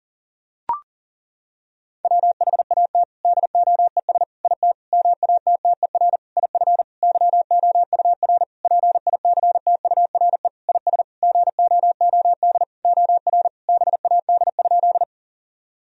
Morse code, heard as a keyed tone, about 40 wpm, 700 Hz. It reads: WHAT DOES IT MATTER IF YOUR PICTURE IS GOOD OR BAD?
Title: Sentences from Top 1000 Words - Review Part 1 - 40wpm